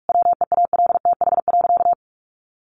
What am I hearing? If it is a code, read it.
Morse code, heard as a keyed tone, about 44 wpm, 700 Hz. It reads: WEALTH.
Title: Sentences from Top 1200 Words 45wpm